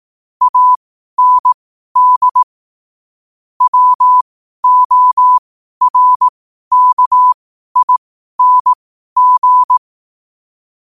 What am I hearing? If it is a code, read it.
Morse code, heard as a keyed tone, about 18 wpm, spaced as for 12 wpm, 1000 Hz. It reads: AND WORKING
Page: Quotes for Thu, 14 Aug 2025 in Morse Code at 12 words per minute.